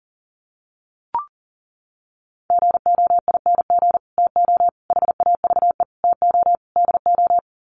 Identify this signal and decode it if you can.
Morse code, heard as a keyed tone, about 40 wpm, 700 Hz. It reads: GOING TO HAVE TO DO
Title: Sets of 5 Words Encore 40wpm